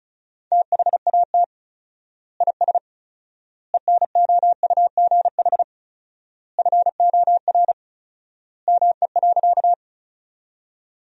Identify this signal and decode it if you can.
Morse code, heard as a keyed tone, about 34 wpm, 700 Hz. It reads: THAT IS ENOUGH FOR ME.
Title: 4x Word-Spacing - Sentences from Top 400 Words - 35wpm